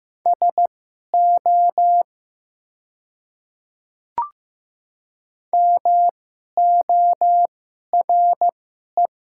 Morse code, heard as a keyed tone, about 15 wpm, spaced as for 11 wpm, 700 Hz. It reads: SO MORE
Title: 2x Character Spacing - Top 100 Words - 15wpm